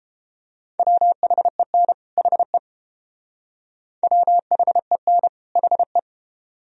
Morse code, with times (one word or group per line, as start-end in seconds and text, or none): 0.8-1.9 WHEN
2.2-2.6 HE
4.0-5.3 WHEN
5.6-6.0 HE